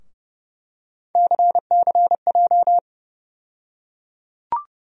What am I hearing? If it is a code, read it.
Morse code, heard as a keyed tone, about 30 wpm, 700 Hz. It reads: CCJ
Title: USA Call Sign Postfixes - 3 Letters 30wpm